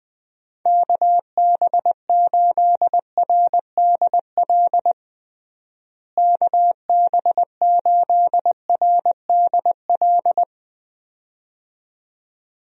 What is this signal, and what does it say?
Morse code, heard as a keyed tone, about 20 wpm, 700 Hz. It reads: KB8RDL KB8RDL
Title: USA Call Signs - 6 Characters - Repeated 2x - 20wpm